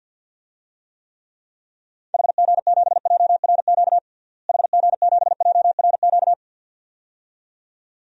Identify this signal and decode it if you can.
Morse code, heard as a keyed tone, about 53 wpm, 700 Hz. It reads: HG7JRQ HG7JRQ
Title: DXCC-100 Call Signs - 6 Characters - Repeated 2x - 50wpm